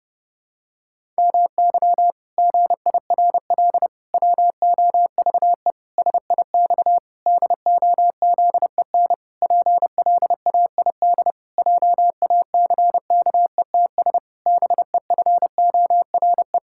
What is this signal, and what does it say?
Morse code, heard as a keyed tone, about 30 wpm, 700 Hz. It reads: MY GIRL WOVE SIX DOZEN PLAID JACKETS BEFORE
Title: Pangram Challenge v2 - 30wpm